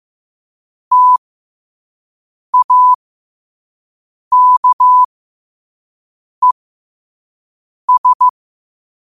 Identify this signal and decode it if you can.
Morse code, heard as a keyed tone, about 15 wpm, spaced as for 5 wpm, 1000 Hz. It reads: TAKES
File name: qotdfiles / 5wpm